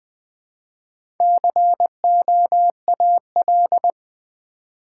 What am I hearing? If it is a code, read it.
Morse code, heard as a keyed tone, about 20 wpm, 700 Hz. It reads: COAL
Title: Build A Word - 1 Letter At A Time - 3 to 5-Letter Words - 20wpm